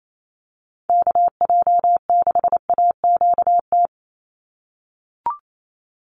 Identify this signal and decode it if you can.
Morse code, heard as a keyed tone, about 28 wpm, 700 Hz. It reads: KJ6AQT
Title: USA Call Signs 28wpm